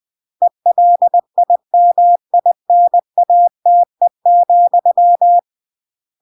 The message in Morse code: ELIMINATE,